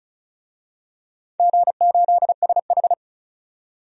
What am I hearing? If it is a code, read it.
Morse code, heard as a keyed tone, about 34 wpm, 700 Hz. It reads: G8SH
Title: DXCC-100 Call Signs - 4 Characters - 35wpm